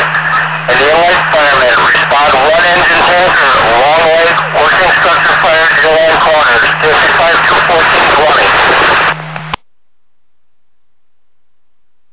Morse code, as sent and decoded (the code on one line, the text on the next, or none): none